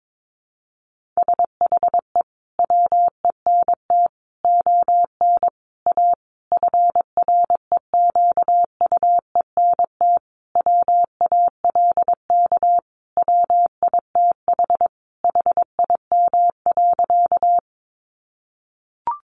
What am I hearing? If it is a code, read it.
Morse code, heard as a keyed tone, about 22 wpm, 700 Hz. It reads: SHE WENT ON A FREQUENT WALK WITH HIM.
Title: Sentences from Top 1400 Words 22wpm